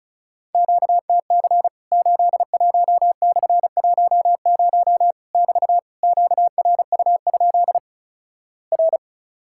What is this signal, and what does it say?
Morse code, heard as a keyed tone, about 36 wpm, 700 Hz. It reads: QTC 81/10 <BT> QRU? R
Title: Worked All Europe DX Contest - QTC Exchange - 35wpm